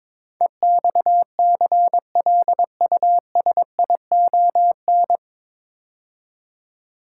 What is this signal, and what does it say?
Morse code, heard as a keyed tone, about 22 wpm, 700 Hz. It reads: EXCLUSION
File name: 9 Letter Words - First 3 Letters Given - 22wpm